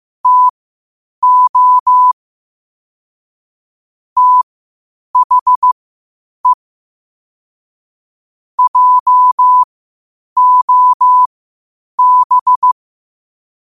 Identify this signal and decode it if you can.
Morse code, heard as a keyed tone, about 15 wpm, spaced as for 8 wpm, 1000 Hz. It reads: TO THE JOB
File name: qotdfiles / 8wpm